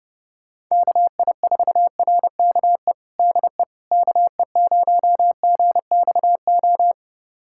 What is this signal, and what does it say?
Morse code, heard as a keyed tone, about 30 wpm, 700 Hz. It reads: KI4RKE DE KE0GXO
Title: USA Callsign DE Callsign - 6 Characters - 30wpm